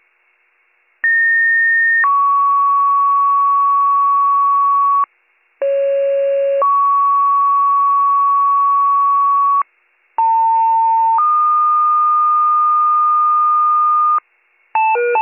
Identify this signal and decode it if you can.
Two-tone page, heard as a one-second tone, then a three-second tone, about 1850 then 1120 Hz, about 550 then 1120 Hz, about 900 then 1220 Hz. It is heard from 1.0 to 14.2 s.